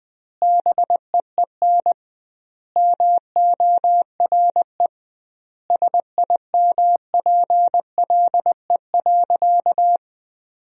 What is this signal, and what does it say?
Morse code, heard as a keyed tone, about 20 wpm, 700 Hz. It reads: BEEN MORE SIMPLE.
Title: 2x Word-Spacing - Sentences from Top 400 Words - 20wpm